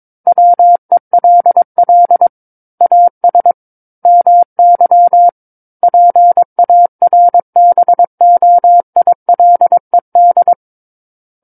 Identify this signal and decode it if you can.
Morse code, heard as a keyed tone, about 22 wpm, 700 Hz. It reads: WELL AS MY PARBOILED
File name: Book One / 13-Chapter 13 - 22 WPM